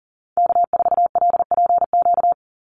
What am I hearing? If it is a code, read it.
Morse code, heard as a keyed tone, about 39 wpm, 700 Hz. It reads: K4LPQ
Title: NA SOTA Activator Callsigns 40wpm